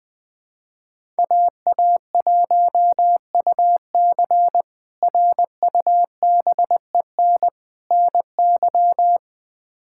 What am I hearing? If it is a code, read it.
Morse code, heard as a keyed tone, about 20 wpm, 700 Hz. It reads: AA1UC RUBEN NY
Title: K1USN SST QSO Practice - 20wpm